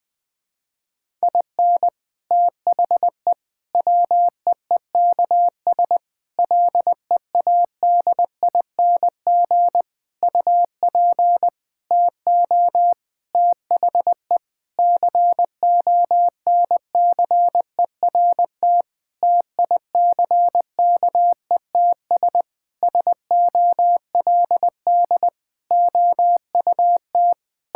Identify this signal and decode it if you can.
Morse code, heard as a keyed tone, about 21 wpm, 700 Hz. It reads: IN THE WEEKS LEADING UP TO THE CONCERT TICKETS SOLD OUT
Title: Sentences with 6-Word Phrases - Spoken Beginning - 20wpm